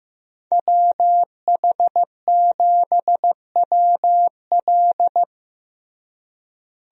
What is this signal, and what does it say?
Morse code, heard as a keyed tone, about 16 wpm, 700 Hz. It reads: WH7WL
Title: USA Call Signs - 5 Characters - Repeated 3x - 15wpm